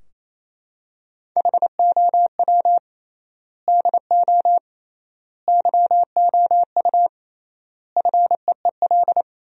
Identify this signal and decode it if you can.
Morse code, heard as a keyed tone, about 28 wpm, 700 Hz. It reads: HOW DO YOU FEEL
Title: 3x Word-Spacing - Sentences from Top 400 Words - 28wpm